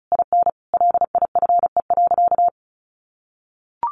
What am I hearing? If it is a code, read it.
Morse code, heard as a keyed tone, about 34 wpm, 700 Hz. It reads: IN LIFE.
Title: Sentences from Top 1200 Words 35wpm